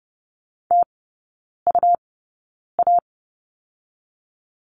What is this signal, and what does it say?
Morse code, heard as a keyed tone, about 30 wpm, spaced as for 9 wpm, 700 Hz. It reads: TUA
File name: Lesson 17e - Letter u - Three Characters 30wpm